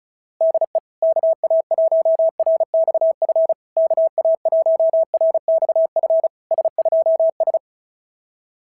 Morse code, heard as a keyed tone, about 36 wpm, 690 Hz. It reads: DE KA1RXF KA1RXF S2S
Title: SOTA QSO Practice 35wpm